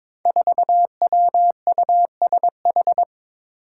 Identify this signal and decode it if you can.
Morse code, heard as a keyed tone, about 22 wpm, 700 Hz. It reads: 4WUSH
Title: Five Challenging Letter-Number Combinations 22wpm